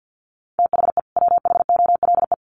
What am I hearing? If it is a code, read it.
Morse code, heard as a keyed tone, about 48 wpm, 700 Hz. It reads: THE WHOLE
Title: Sets of 2 Words 50wpm